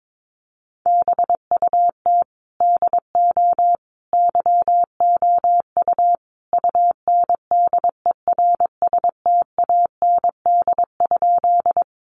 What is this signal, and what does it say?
Morse code, heard as a keyed tone, about 22 wpm, 700 Hz. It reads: BUT DO YOU UNDERSTAND?
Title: Sentences from Top 600 Words 22wpm